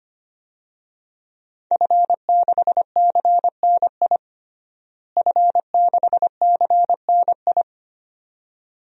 Morse code, heard as a keyed tone, about 25 wpm, 700 Hz. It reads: F6CNI F6CNI
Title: DXCC-100 Call Signs - 5 Characters - Repeated 2x - 25wpm